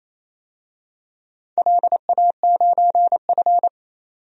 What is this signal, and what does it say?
Morse code, heard as a keyed tone, about 28 wpm, 700 Hz. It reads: LA9F
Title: DXCC-100 Call Signs - 4 Characters - 28wpm